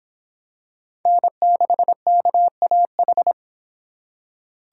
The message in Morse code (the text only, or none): N6KAH